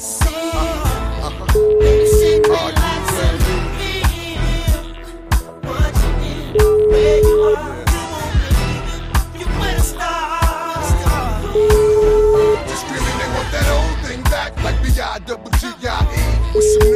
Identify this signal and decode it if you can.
Ringback tone (Europe, 425 Hz), with noise.